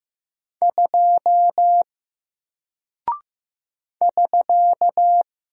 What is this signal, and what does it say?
Morse code, heard as a keyed tone, about 15 wpm, 700 Hz. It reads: 2 <SK>
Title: Single Character (Letter:Number:Punctuation:Prosign) Alt 15wpm